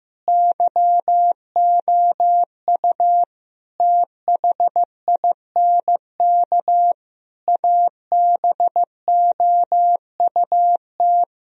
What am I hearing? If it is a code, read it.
Morse code, heard as a keyed tone, about 16 wpm, 700 Hz. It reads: YOU THINK ABOUT
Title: Sentences from Top 1000 Words - Review Part 4 - 15wpm